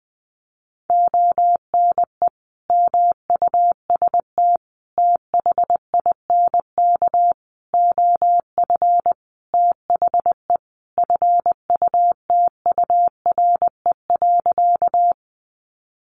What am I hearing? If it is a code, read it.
Morse code, heard as a keyed tone, about 20 wpm, 700 Hz. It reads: ONE MUST THINK OF THE FUTURE.
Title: Sentences from Top 700 Words 20wpm